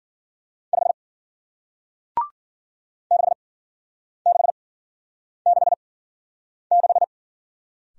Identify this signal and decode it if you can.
Morse code, heard as a keyed tone, about 51 wpm, spaced as for 9 wpm, 700 Hz. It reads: 5 6666